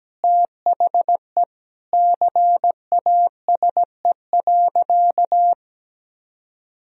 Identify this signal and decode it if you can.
Morse code, heard as a keyed tone, about 17 wpm, 700 Hz. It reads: THE CASE.
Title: Sentences from Top 1000 Words - Review Part 4 - 17wpm